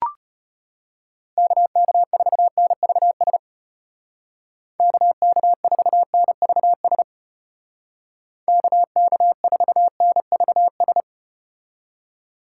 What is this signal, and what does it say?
Morse code, heard as a keyed tone, about 34 wpm, 700 Hz. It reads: KK4NVS KK4NVS KK4NVS